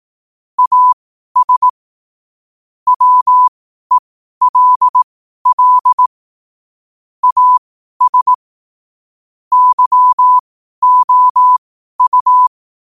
Morse code, heard as a keyed tone, about 18 wpm, spaced as for 12 wpm, 1000 Hz. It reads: AS WELL AS YOU
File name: qotdfiles / 12wpm